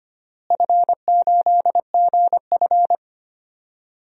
Morse code, heard as a keyed tone, about 25 wpm, 700 Hz. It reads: F8GF
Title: DXCC-100 Call Signs - 4 Characters - Repeated 3x - 25wpm